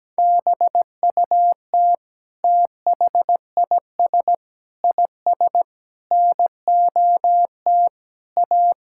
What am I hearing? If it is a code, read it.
Morse code, heard as a keyed tone, about 17 wpm, 700 Hz. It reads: BUT THIS IS NOT A
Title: Sentences from Top 1600 Words 17wpm